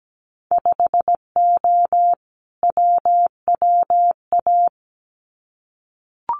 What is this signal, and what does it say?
Morse code, heard as a keyed tone, about 17 wpm, 700 Hz. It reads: 5O WWA
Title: Winter Field Day Exchanges Encore 17wpm